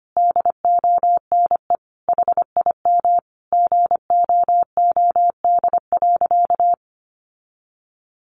Morse code, heard as a keyed tone, about 25 wpm, 700 Hz. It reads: DONE HIM GOOD.
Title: Sentences from Top 1200 Words 25wpm